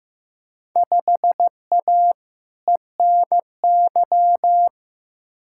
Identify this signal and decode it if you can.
Morse code, heard as a keyed tone, about 15 wpm, 700 Hz. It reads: 5A ENY
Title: Field Day Exchanges Encore - 15wpm